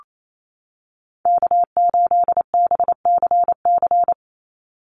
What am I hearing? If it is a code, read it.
Morse code, heard as a keyed tone, about 28 wpm, 700 Hz. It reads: K8BCC
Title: USA Call Signs 28wpm